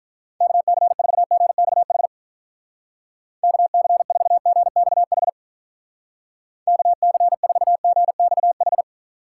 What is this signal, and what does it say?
Morse code, heard as a keyed tone, about 45 wpm, 700 Hz. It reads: KC4GXH KC4GXH KC4GXH